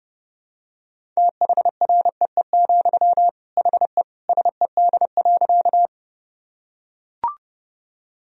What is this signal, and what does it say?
Morse code, heard as a keyed tone, about 32 wpm, 700 Hz. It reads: THREE, HE SED.
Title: Sentences from Top 200 Words and Abbreviations 30wpm